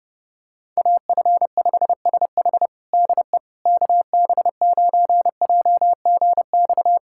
Morse code, heard as a keyed tone, about 32 wpm, 700 Hz. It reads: AF5SH DE KB9JGX
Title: USA Callsign DE Callsign 30wpm